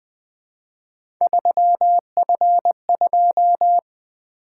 Morse code, heard as a keyed tone, about 20 wpm, 700 Hz. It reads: 3F2
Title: Three Letter-Number Combinations - 20wpm